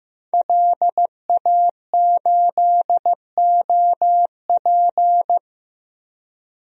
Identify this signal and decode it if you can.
Morse code, heard as a keyed tone, about 15 wpm, 700 Hz. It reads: LA8OP
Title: DXCC-100 Call Signs - 5 Characters - Repeated 3x - 15wpm